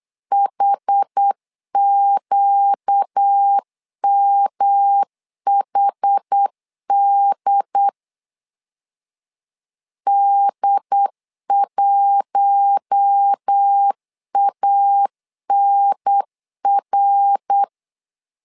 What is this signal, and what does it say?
Morse code, heard as a keyed tone, about 8 wpm, 800 Hz. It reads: HQMHD D1ANR
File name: LEZIONI / 4 Q1 8wpm 3035